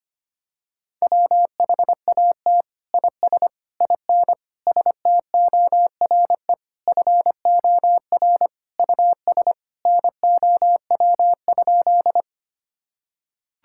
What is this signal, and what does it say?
Morse code, heard as a keyed tone, about 25 wpm, 700 Hz. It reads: WHAT IS IN STORE FOR US NOW?
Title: Sentences from Top 1000 Words - Review Part 5 - 25wpm